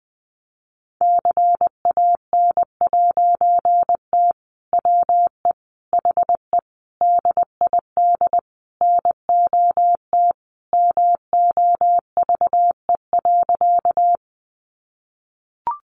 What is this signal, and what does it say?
Morse code, heard as a keyed tone, about 19 wpm, 700 Hz. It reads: CAN'T WE HE DID NOT MOVE.
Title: A Sound Of Thunder 20wpm